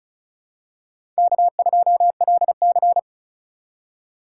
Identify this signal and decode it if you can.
Morse code, heard as a keyed tone, about 36 wpm, 700 Hz. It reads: K2LC
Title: USA Call Signs - 4 Characters - 6x Character Spacing - 35wpm